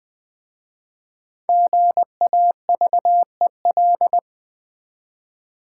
Morse code, heard as a keyed tone, about 20 wpm, 700 Hz. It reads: GAVEL
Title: Easy Words with Hints - 20wpm